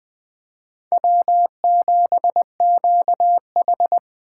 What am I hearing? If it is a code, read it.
Morse code, heard as a keyed tone, about 20 wpm, 700 Hz. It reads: W7QH
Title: USA Call Signs - 4 Characters - 8x Character Spacing - 20wpm